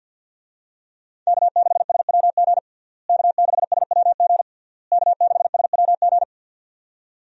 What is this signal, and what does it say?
Morse code, heard as a keyed tone, about 50 wpm, 700 Hz. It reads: K6SWG K6SWG K6SWG
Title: USA Call Signs - 5 Characters - Repeated 3x - 50wpm